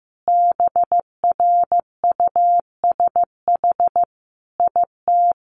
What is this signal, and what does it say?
Morse code, heard as a keyed tone, about 15 wpm, 700 Hz. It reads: BRUSH IT
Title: Sentences from Top 1100 Words 15wpm